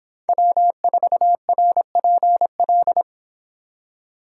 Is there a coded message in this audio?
Yes: Morse code from 0.3 to 3.0 s, heard as a keyed tone, about 26 wpm, 700 Hz. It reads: W4RPL